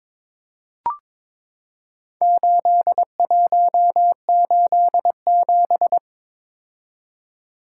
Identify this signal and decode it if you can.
Morse code, heard as a keyed tone, about 23 wpm, 700 Hz. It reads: 8187
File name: practice / 22wpm / Four Letter-Number Combinations - 22wpm